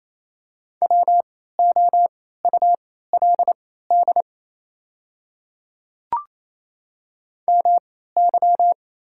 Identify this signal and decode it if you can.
Morse code, heard as a keyed tone, about 28 wpm, spaced as for 16 wpm, 690 Hz. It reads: WOULD MY
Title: 3x Character Spacing - Top 100 Words - 28wpm